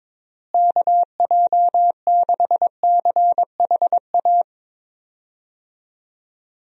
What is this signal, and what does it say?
Morse code, heard as a keyed tone, about 22 wpm, 700 Hz. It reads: KJ6CHA
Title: USA Call Signs - 6 Characters - Repeated 2x - 22wpm